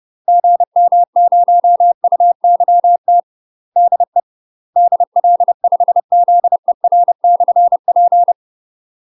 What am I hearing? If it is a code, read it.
Morse code, heard as a keyed tone, about 30 wpm, 700 Hz. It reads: GM0UYT DE DL5ZER/P
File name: UK MASTER QSOs TRAINING / 30WPM / UK MASTER QSOs 0030